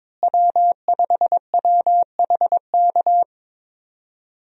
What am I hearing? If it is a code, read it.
Morse code, heard as a keyed tone, about 22 wpm, 700 Hz. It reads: W5WHK
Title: USA Call Signs - 5 Characters - Repeated 3x - 22wpm